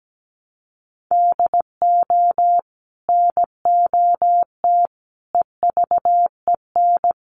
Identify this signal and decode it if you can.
Morse code, heard as a keyed tone, about 17 wpm, 700 Hz. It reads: DO NOT EVEN
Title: Sets of 4 Words 17wpm